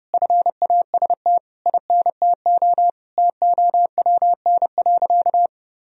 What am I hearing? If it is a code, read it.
Morse code, heard as a keyed tone, about 30 wpm, 700 Hz. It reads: FAST INTO TOWN.
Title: Sentences from Top 1000 Words - Review Part 2 - 30wpm